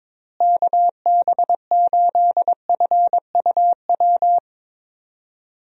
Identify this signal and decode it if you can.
Morse code, heard as a keyed tone, about 22 wpm, 700 Hz. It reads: KB8FUW
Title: USA Call Signs - 6 Characters - Repeated 3x - 22wpm